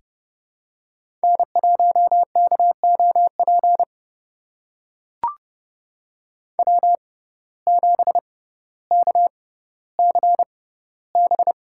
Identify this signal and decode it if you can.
Morse code, heard as a keyed tone, about 31 wpm, spaced as for 16 wpm, 700 Hz. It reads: N1KOP W7KCB
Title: USA Call Signs - 5 Characters - 6x Character Spacing - 30wpm